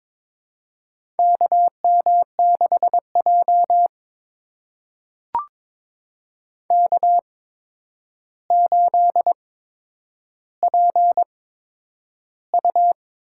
Morse code, heard as a keyed tone, about 23 wpm, spaced as for 10 wpm, 700 Hz. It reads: KM6J K8PU
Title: USA Call Signs - 4 Characters - 8x Character Spacing - 22wpm